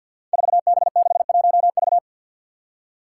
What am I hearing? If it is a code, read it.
Morse code, heard as a keyed tone, about 53 wpm, 700 Hz. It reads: 4B61V
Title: Five Challenging Letter-Number Combinations 50wpm